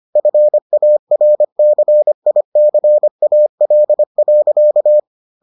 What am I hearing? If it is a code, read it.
Morse code, heard as a keyed tone, about 25 wpm, 600 Hz. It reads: FARCICAL.